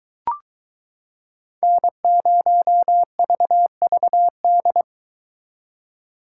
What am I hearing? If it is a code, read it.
Morse code, heard as a keyed tone, about 23 wpm, 700 Hz. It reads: N0VVD